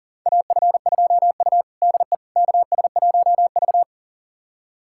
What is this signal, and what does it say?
Morse code, heard as a keyed tone, about 37 wpm, 700 Hz. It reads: AF2U DE KS1V
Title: USA Callsign DE Callsign - 4 Characters - 40wpm